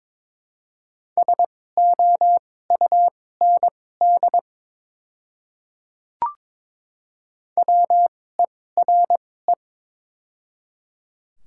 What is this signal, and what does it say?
Morse code, heard as a keyed tone, about 23 wpm, spaced as for 16 wpm, 700 Hz. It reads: SOUND WERE
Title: 2x Character Spacing - Top 100 Words - 22wpm